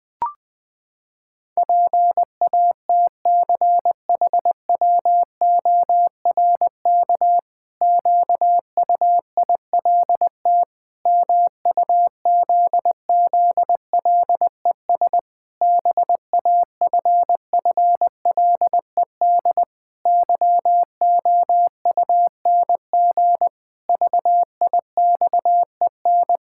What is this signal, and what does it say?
Morse code, heard as a keyed tone, about 20 wpm, 700 Hz. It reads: PATCHWORK QUILT MUZZLES BAFFLED YOUNG VIXEN
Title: Pangram Challenge v2 - 20wpm